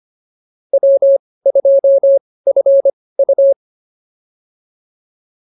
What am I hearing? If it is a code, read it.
Morse code, heard as a keyed tone, about 25 wpm, spaced as for 18 wpm, 550 Hz. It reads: W2FU